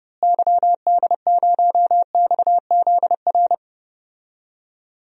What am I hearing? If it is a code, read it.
Morse code, heard as a keyed tone, about 32 wpm, 700 Hz. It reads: YD0XZR
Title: DXCC-100 Call Signs - 6 Characters - Repeated 2x - 30wpm